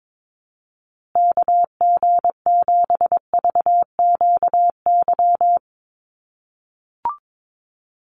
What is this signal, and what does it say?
Morse code, heard as a keyed tone, about 22 wpm, 700 Hz. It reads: KG7VQY
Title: USA Call Signs Repeated 3x 22wpm